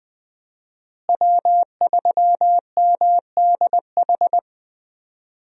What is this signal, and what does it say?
Morse code, heard as a keyed tone, about 21 wpm, 700 Hz. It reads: W3MDH
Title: USA Call Signs - 5 Characters - Repeated 3x - 20wpm